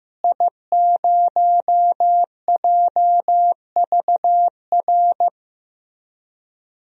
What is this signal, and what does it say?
Morse code, heard as a keyed tone, about 15 wpm, 700 Hz. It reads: I0JVR
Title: DXCC-100 Call Signs - 5 Characters - Repeated 3x - 15wpm